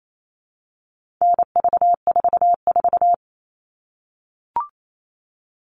Morse code, heard as a keyed tone, about 28 wpm, 700 Hz. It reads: NV44
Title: All Cut Numbers - 4-Digit 28wpm